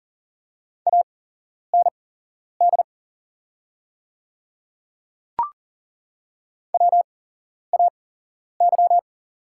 Morse code, heard as a keyed tone, about 40 wpm, spaced as for 11 wpm, 700 Hz. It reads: AND WAY
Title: 8x Character Spacing - Top 100 Words - 40wpm